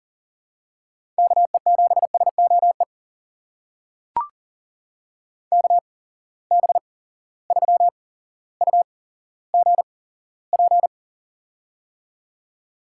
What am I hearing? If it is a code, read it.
Morse code, heard as a keyed tone, about 40 wpm, spaced as for 17 wpm, 700 Hz. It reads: KE7SOE KB3UGP
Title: USA Call Signs - 6 Characters - 8x Character Spacing - 40wpm